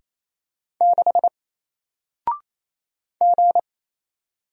Morse code, heard as a keyed tone, about 28 wpm, 700 Hz. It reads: 6 G
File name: Single-Letter-Number-28wpm